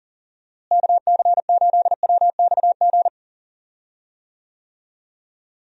Morse code, heard as a keyed tone, about 40 wpm, 700 Hz. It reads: KC8WXG
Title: USA Call Signs - 6 Characters - Repeated 2x - 40wpm